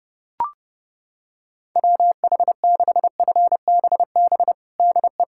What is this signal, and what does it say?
Morse code, heard as a keyed tone, about 30 wpm, 700 Hz. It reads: WH6FBB DE
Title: USA Callsign DE Callsign 30wpm